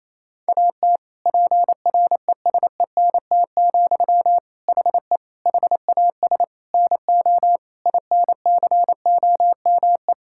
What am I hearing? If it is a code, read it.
Morse code, heard as a keyed tone, about 28 wpm, 700 Hz. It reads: AT PRESENT, HE HAS NO INCOME
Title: Sentences from Top 1000 Words - Review Part 5 - 28wpm